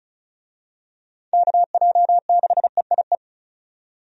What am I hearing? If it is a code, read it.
Morse code, heard as a keyed tone, about 36 wpm, 700 Hz. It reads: KJ6EIE
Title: USA Call Signs - 6 Characters - 35wpm